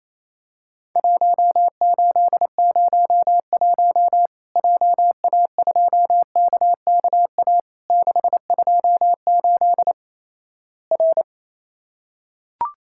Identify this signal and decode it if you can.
Morse code, heard as a keyed tone, about 28 wpm, 700 Hz. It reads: 1801 JA2KKA 628 R
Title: Worked All Europe DX Contest - QTC Exchange - 28wpm